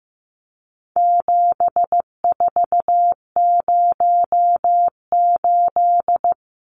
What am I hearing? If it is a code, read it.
Morse code, heard as a keyed tone, about 15 wpm, 700 Hz. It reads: 7408
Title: Numbers - 4-Digit 15wpm